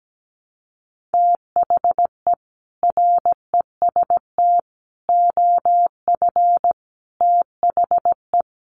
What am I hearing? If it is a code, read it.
Morse code, heard as a keyed tone, about 17 wpm, 700 Hz. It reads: THE REST OF THE
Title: Sets of 5 Words Encore 17wpm